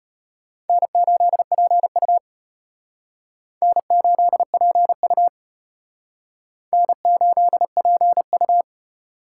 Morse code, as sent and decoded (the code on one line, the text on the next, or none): -. ---.. .--. ..- / -. ---.. .--. ..- / -. ---.. .--. ..-
N8PU N8PU N8PU